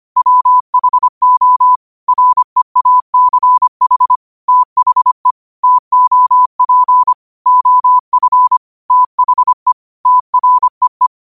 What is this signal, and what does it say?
Morse code, heard as a keyed tone, about 25 wpm, 1000 Hz. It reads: WHO REACH THE TOP OF THE TREE
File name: qotdfiles / 25wpm